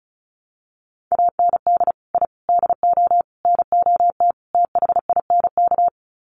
Morse code, heard as a keyed tone, about 34 wpm, 700 Hz. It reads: AND I DO NOT THINK
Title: Sets of 5 Words 35wpm